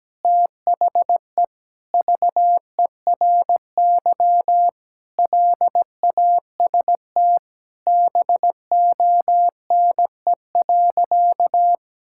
Morse code, heard as a keyed tone, about 17 wpm, 700 Hz. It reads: THE VERY LAST BONE.
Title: Sentences from Top 1600 Words 17wpm